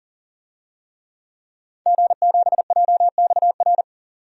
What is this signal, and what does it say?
Morse code, heard as a keyed tone, about 40 wpm, 700 Hz. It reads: G7JXR
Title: DXCC-100 Call Signs - 5 Characters - 40wpm